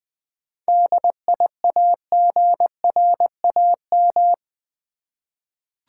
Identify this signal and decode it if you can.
Morse code, heard as a keyed tone, about 20 wpm, 700 Hz. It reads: DIAGRAM
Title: 7 Letter Words - First 3 Letters Given - 20wpm